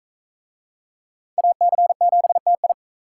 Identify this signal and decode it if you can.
Morse code, heard as a keyed tone, about 39 wpm, 700 Hz. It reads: AC7TI